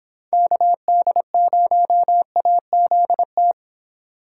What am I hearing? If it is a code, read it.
Morse code, heard as a keyed tone, about 26 wpm, 700 Hz. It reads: KD0AZT